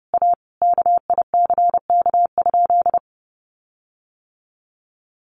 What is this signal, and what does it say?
Morse code, heard as a keyed tone, about 28 wpm, 700 Hz. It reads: A KICK?
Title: Sentences from Top 1300 Words 30wpm